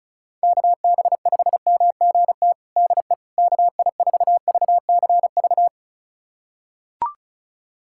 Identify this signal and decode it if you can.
Morse code, heard as a keyed tone, about 34 wpm, 700 Hz. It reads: KB5MGT DE KI4VCV
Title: USA Callsign DE Callsign - 6 Characters - 35wpm